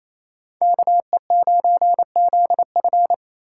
Morse code, heard as a keyed tone, about 36 wpm, 700 Hz. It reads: KE9ZF